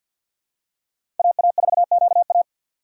Morse code, heard as a keyed tone, about 51 wpm, 700 Hz. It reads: AA4QA
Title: USA Call Signs - 5 Characters - 2x Character Spacing - 50wpm